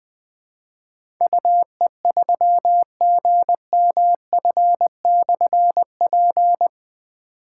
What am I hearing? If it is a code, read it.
Morse code, heard as a keyed tone, about 20 wpm, 700 Hz. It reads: UE3GMF/P
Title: Difficult Call Signs 20wpm